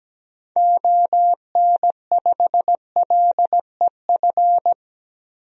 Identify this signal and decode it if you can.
Morse code, heard as a keyed tone, about 17 wpm, 700 Hz. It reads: ON5LEF
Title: DXCC-100 Call Signs - 6 Characters - 17wpm